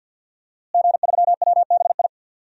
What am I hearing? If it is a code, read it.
Morse code, heard as a keyed tone, about 47 wpm, 700 Hz. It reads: G3WBI
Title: DXCC-100 Call Signs - 5 Characters - Repeated 3x - 50wpm